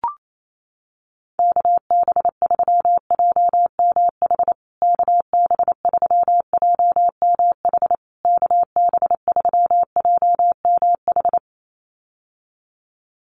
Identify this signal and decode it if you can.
Morse code, heard as a keyed tone, about 28 wpm, 700 Hz. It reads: KB3JMH KB3JMH KB3JMH
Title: USA Call Signs Repeated 3x 28wpm